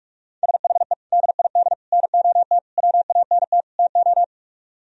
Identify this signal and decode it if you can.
Morse code, heard as a keyed tone, about 49 wpm, 700 Hz. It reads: SHE DID NOT WANT TO